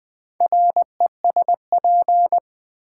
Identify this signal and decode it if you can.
Morse code, heard as a keyed tone, about 20 wpm, 700 Hz. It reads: RESP
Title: Build A Word - 4 Letters At A Time - 11-Letter Words - 20wpm